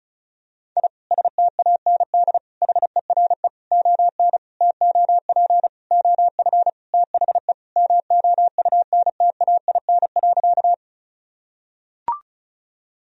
Morse code, heard as a keyed tone, about 36 wpm, 700 Hz. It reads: I STAND HERE ON TOP OF THE MOUNTAIN.
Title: Sentences from Top 1000 Words - Review Part 1 - 35wpm